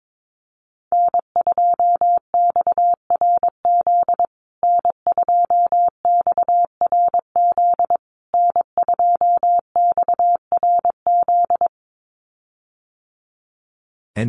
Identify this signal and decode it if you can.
Morse code, heard as a keyed tone, about 22 wpm, 700 Hz. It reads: N2XRZ N2XRZ N2XRZ
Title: USA Call Signs Repeated 3x 22wpm